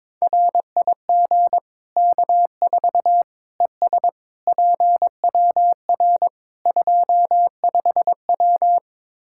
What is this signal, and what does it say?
Morse code, heard as a keyed tone, about 21 wpm, 700 Hz. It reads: RIG K4 ES PWR 25W
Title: Practice QSO - Exchange 2 - Info - 22wpm